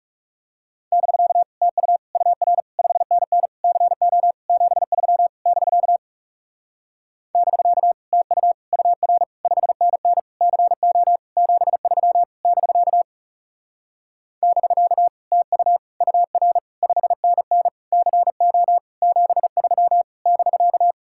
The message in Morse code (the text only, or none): <BK> TU UR 5NN CO 73 <BK> <BK> TU UR 5NN CO 73 <BK> <BK> TU UR 5NN CO 73 <BK>